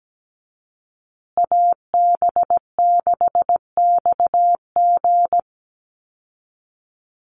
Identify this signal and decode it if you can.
Morse code, heard as a keyed tone, about 17 wpm, 700 Hz. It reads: AB6XG
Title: Call Signs Repeated 2x - 17wpm